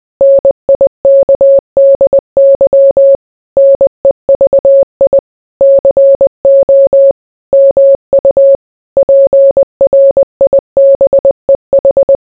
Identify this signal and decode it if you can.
Morse code, heard as a keyed tone, about 20 wpm, 550 Hz. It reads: NIKDY NEVI CO MU PRIBEH